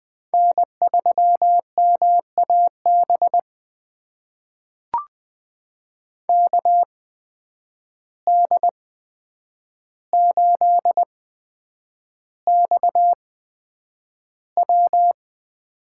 Morse code, heard as a keyed tone, about 20 wpm, spaced as for 9 wpm, 700 Hz. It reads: N3MAB KD8XW
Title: USA Call Signs - 5 Characters - 8x Character Spacing - 20wpm